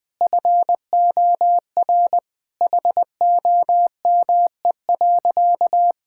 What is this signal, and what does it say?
Morse code, heard as a keyed tone, about 20 wpm, 700 Hz. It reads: FOR HOME.
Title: Sentences with 4-Word Phrases - 20wpm